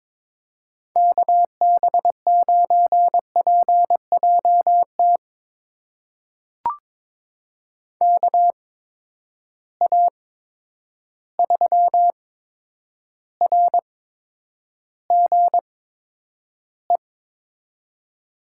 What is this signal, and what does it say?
Morse code, heard as a keyed tone, about 22 wpm, spaced as for 9 wpm, 700 Hz. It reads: KB9PJT KA3RGE
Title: USA Call Signs - 6 Characters - 8x Character Spacing - 22wpm